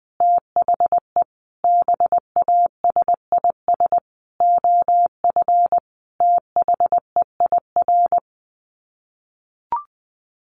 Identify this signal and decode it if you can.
Morse code, heard as a keyed tone, about 20 wpm, 700 Hz. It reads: THE BASIS OF THEIR
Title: Sets of 5 Words Encore 20wpm